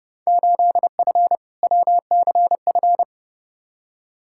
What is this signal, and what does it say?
Morse code, heard as a keyed tone, about 31 wpm, 700 Hz. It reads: 8F WCF
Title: Field Day Exchanges Encore - 30wpm